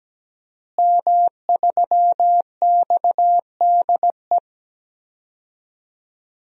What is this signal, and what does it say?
Morse code, heard as a keyed tone, about 17 wpm, 700 Hz. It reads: M3XDE
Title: DXCC-100 Call Signs - 5 Characters - Repeated 2x - 17wpm